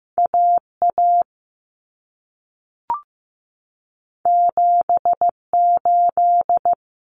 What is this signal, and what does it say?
Morse code, heard as a keyed tone, about 15 wpm, 700 Hz. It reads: AA 78
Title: Common Cut Numbers - 2-Digit 15wpm